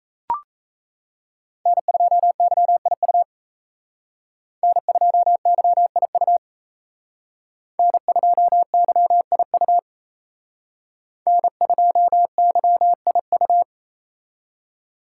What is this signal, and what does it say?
Morse code, heard as a keyed tone, about 35 wpm, 700 Hz. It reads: N2YIU N2YIU N2YIU N2YIU
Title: Speed Racing - USA 1x3 Call Signs 28wpm